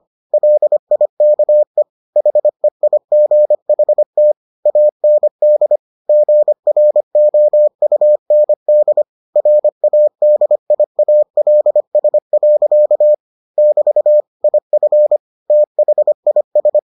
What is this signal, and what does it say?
Morse code, heard as a keyed tone, about 25 wpm, 600 Hz. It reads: LIKE HEIGHT AND GROUND RADIALS. <BT> IF THIS